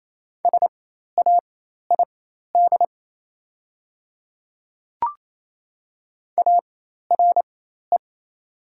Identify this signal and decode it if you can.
Morse code, heard as a keyed tone, about 28 wpm, spaced as for 13 wpm, 700 Hz. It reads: SAID ARE
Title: 4x Character Spacing - Top 100 Words - 28wpm